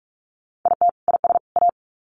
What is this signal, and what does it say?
Morse code, heard as a keyed tone, about 44 wpm, 700 Hz. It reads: IT IS A
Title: Sets of 3 Words 45wpm